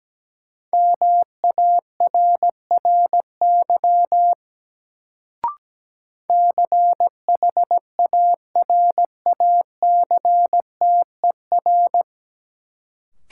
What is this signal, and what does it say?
Morse code, heard as a keyed tone, about 17 wpm, 700 Hz. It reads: MARRY CHARACTER
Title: Sentences from Top 1200 Words - New Word Drill Alt - 17wpm